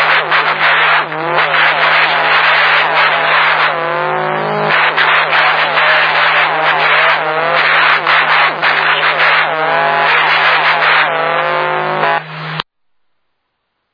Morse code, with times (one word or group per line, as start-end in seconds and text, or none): none